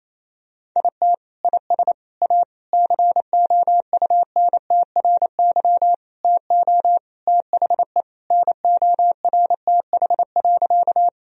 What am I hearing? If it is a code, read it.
Morse code, heard as a keyed tone, about 28 wpm, 700 Hz. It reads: IT IS A COUNTRY TO THE NORTH.
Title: Sentences from Top 1000 Words - Review Part 1 - 28wpm